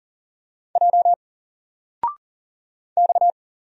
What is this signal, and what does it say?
Morse code, heard as a keyed tone, about 40 wpm, 700 Hz. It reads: J X
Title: Single Letter-Number Alt 40wpm